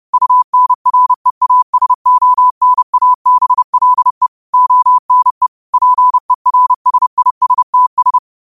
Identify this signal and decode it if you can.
Morse code, heard as a keyed tone, about 30 wpm, 1000 Hz. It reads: UNREASONABLE ONE PERSISTS